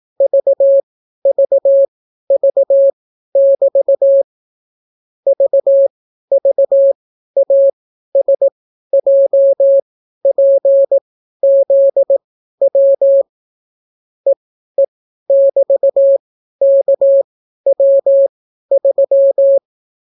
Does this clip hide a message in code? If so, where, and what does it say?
Yes: Morse code from 0.2 to 19.6 s, heard as a keyed tone, about 18 wpm, spaced as for 12 wpm, 550 Hz. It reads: VVV<BT> VVASJPZW EE<BT>KW3